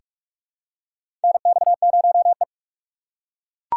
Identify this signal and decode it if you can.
Morse code, heard as a keyed tone, about 42 wpm, 700 Hz. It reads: NX0E
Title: USA Call Signs - 6x Character Spacing - 45wpm